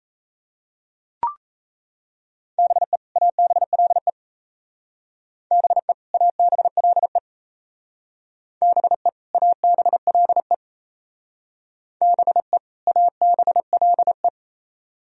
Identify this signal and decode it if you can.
Morse code, heard as a keyed tone, about 35 wpm, 700 Hz. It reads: BE ABLE BE ABLE BE ABLE BE ABLE
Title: Speed-Racing - Sets of 2 Words 28wpm